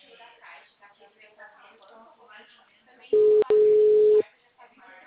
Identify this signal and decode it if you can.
Ringback tone (Europe, 425 Hz), with speech.